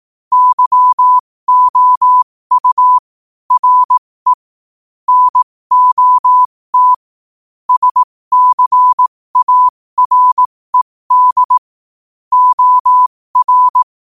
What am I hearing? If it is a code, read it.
Morse code, heard as a keyed tone, about 18 wpm, spaced as for 15 wpm, 1000 Hz. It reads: YOU RE NOT SCARED OR
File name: qotdfiles / 15wpm